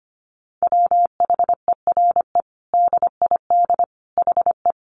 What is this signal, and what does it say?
Morse code, heard as a keyed tone, about 25 wpm, 700 Hz. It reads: WHERE DID HE
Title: Sentences from Top 200 Words 25wpm